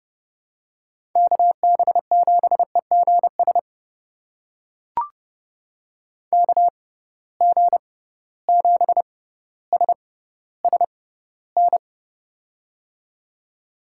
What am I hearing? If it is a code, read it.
Morse code, heard as a keyed tone, about 30 wpm, spaced as for 15 wpm, 700 Hz. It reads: KB7EGS KG7SSN
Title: USA Call Signs - 6x Character Spacing - 30wpm